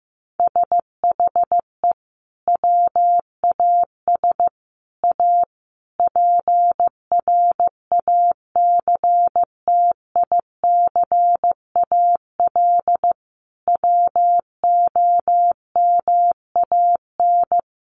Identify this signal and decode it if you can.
Morse code, heard as a keyed tone, about 15 wpm, 700 Hz. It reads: SHE WAS A PRACTICAL WOMAN
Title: Sentences from Top 1400 Words 15wpm